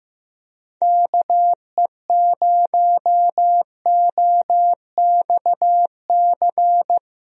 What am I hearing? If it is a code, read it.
Morse code, heard as a keyed tone, about 15 wpm, 700 Hz. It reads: KE0OXC
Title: USA Call Signs - 6 Characters - 15wpm